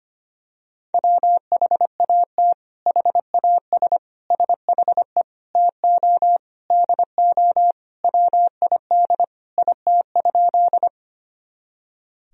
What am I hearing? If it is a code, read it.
Morse code, heard as a keyed tone, about 25 wpm, 700 Hz. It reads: WHAT HAS SHE TO DO WID IT?
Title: Sentences from Top 100 Words and Abbreviations 25wpm